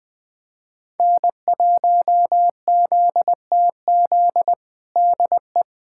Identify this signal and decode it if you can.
Morse code, heard as a keyed tone, about 20 wpm, 700 Hz. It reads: N1ZTZ DE
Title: USA Callsign DE Callsign 20wpm